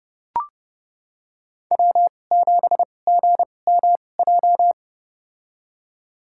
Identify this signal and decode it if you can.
Morse code, heard as a keyed tone, about 30 wpm, spaced as for 22 wpm, 700 Hz. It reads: W7GMJ
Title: USA Call Signs - 5 Characters - 2x Character Spacing - 30wpm